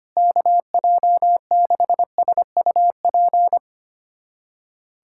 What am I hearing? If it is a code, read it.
Morse code, heard as a keyed tone, about 25 wpm, 700 Hz. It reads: KJ6SUP
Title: USA Callsign DE Callsign 25wpm